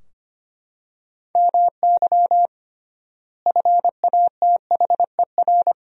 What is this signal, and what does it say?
Morse code, heard as a keyed tone, about 25 wpm, 700 Hz. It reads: MY FATHER
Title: 3x Word-Spacing - Sentences from Top 300 Words - 25wpm